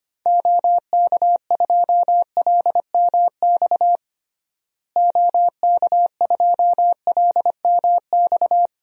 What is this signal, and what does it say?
Morse code, heard as a keyed tone, about 25 wpm, 700 Hz. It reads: OK2LMX OK2LMX
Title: DXCC-100 Call Signs - 6 Characters - Repeated 3x - 25wpm